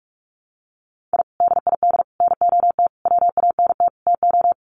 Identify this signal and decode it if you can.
Morse code, heard as a keyed tone, about 44 wpm, 700 Hz. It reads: I DID NOT WANT TO
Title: Sets of 5 Words 45wpm